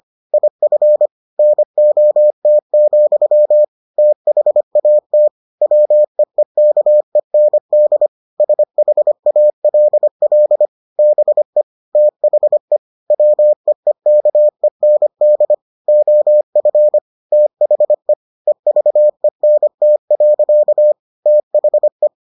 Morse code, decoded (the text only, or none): IF NOT, THAT WEEKEND SHALL BE THE WEEKEND OF THE EVENT. THE